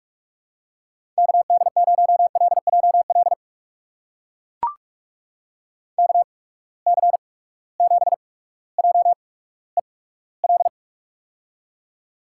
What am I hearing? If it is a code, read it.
Morse code, heard as a keyed tone, about 44 wpm, spaced as for 19 wpm, 700 Hz. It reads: KD0LJL KC7JEL
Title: USA Call Signs - 6 Characters - 8x Character Spacing - 45wpm